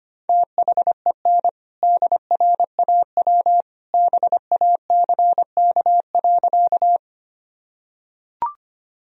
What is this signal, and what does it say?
Morse code, heard as a keyed tone, about 25 wpm, 700 Hz. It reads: THEN DRAW BACK.
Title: Sentences from Top 1000 Words - Review Part 1 - 25wpm